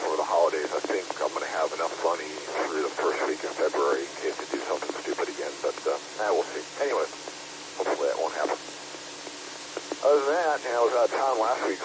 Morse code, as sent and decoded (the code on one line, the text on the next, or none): none